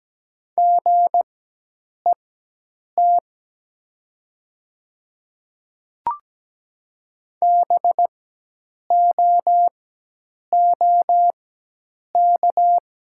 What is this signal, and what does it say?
Morse code, heard as a keyed tone, about 17 wpm, spaced as for 8 wpm, 700 Hz. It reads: GET BOOK
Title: 4x Character Spacing - Top 300 Words - 17wpm